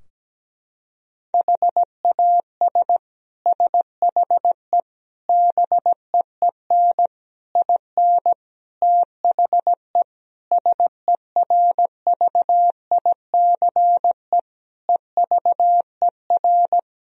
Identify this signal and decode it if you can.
Morse code, heard as a keyed tone, about 17 wpm, 700 Hz. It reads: HAS SHE BEEN IN THE SERVICE EVER
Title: Sentences from Top 1000 Words - Review Part 4 - 17wpm